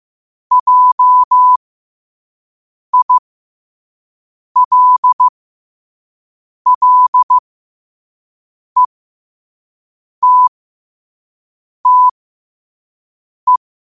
Morse code, heard as a keyed tone, about 15 wpm, spaced as for 5 wpm, 1000 Hz. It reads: JILLETTE
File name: qotdfiles / 5wpm